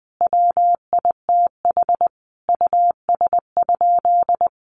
Morse code, heard as a keyed tone, about 20 wpm, 700 Hz. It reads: WITH US?
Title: Sentences from Top 200 Words 20wpm